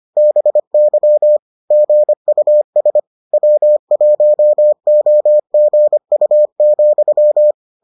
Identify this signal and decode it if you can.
Morse code, heard as a keyed tone, about 25 wpm, 600 Hz. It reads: BY GUS W1OGU,